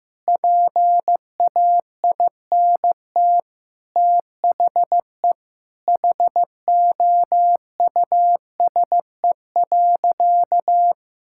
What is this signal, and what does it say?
Morse code, heard as a keyed tone, about 16 wpm, 700 Hz. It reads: PAINT THE HOUSE.
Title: Sentences from Top 1000 Words - Review Part 6 - 15wpm